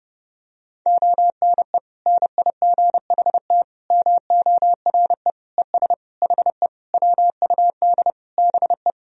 Morse code, heard as a keyed tone, about 30 wpm, 700 Hz. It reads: ONE NIGHT MORE ES HE WUD BE
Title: Sentences from Top 300 Words and Abbreviations 30wpm